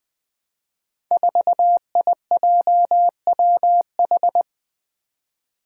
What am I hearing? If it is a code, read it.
Morse code, heard as a keyed tone, about 20 wpm, 700 Hz. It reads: 4IJWH
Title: Five Challenging Letter-Number Combinations 20wpm